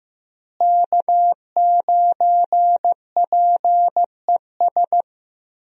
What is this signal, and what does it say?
Morse code, heard as a keyed tone, about 15 wpm, 700 Hz. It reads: K9PES
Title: USA Call Signs - 5 Characters - Repeated 2x - 15wpm